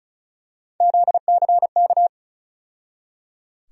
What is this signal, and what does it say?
Morse code, heard as a keyed tone, about 36 wpm, 700 Hz. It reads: ZCK
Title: USA Call Sign Postfixes - 3 Letters 35wpm